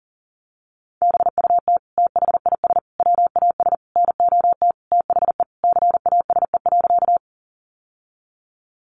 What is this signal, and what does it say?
Morse code, heard as a keyed tone, about 40 wpm, 700 Hz. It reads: BUT THIS WAS NOT THE CASE.
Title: Sentences from Top 700 Words 40wpm